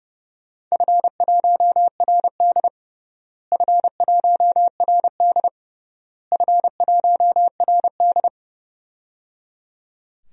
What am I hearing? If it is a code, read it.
Morse code, heard as a keyed tone, about 30 wpm, 700 Hz. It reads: F1RD F1RD F1RD
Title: DXCC-100 Call Signs - 4 Characters - Repeated 3x - 30wpm